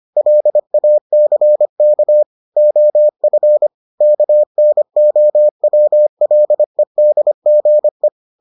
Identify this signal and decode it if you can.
Morse code, heard as a keyed tone, about 25 wpm, 600 Hz. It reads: LACK OF KNOWLEDGE